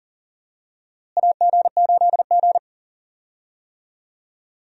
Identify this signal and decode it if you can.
Morse code, heard as a keyed tone, about 40 wpm, 700 Hz. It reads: AG8G
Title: USA Call Signs - 4 Characters - Repeated 3x - 40wpm